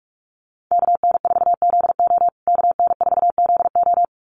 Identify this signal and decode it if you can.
Morse code, heard as a keyed tone, about 44 wpm, 700 Hz. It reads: KN4ZO KN4ZO
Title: USA Call Signs Repeated 2x 45wpm